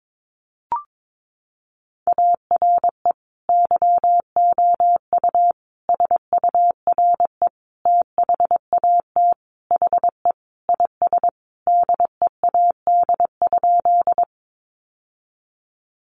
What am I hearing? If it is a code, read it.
Morse code, heard as a keyed tone, about 22 wpm, 700 Hz. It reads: ARE YOU SURE THAT HE IS DEAD?
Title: Sentences from Top 1500 Words 22wpm